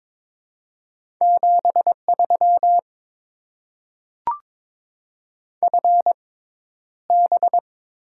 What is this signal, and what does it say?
Morse code, heard as a keyed tone, about 22 wpm, spaced as for 11 wpm, 700 Hz. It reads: 73 FB
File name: Top 20 QSO Elements - 6x Character Spacing - 22wpm